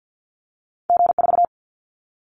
Morse code, heard as a keyed tone, about 48 wpm, 700 Hz. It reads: G4
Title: Callsign Prefixes for Top 25 DXCC - Repeated 3x - 50wpm